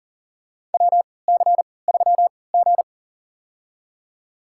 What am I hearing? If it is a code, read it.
Morse code, heard as a keyed tone, about 40 wpm, spaced as for 23 wpm, 700 Hz. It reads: WC3G
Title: USA Call Signs - 4 Characters - 3x Character Spacing - 40wpm